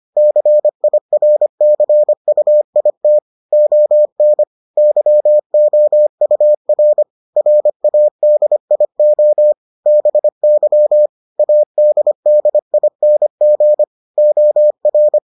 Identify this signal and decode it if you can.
Morse code, heard as a keyed tone, about 25 wpm, 600 Hz. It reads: CIRCUIT ON YOUR RADIO BY ADDING OR